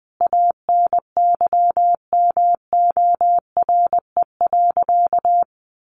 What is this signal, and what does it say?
Morse code, heard as a keyed tone, about 20 wpm, 700 Hz. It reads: ANYMORE.
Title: Additional Easy Phrases 20wpm